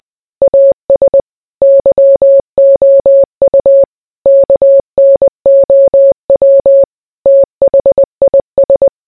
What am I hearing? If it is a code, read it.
Morse code, heard as a keyed tone, about 20 wpm, 550 Hz. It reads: AS YOU KNOW THIS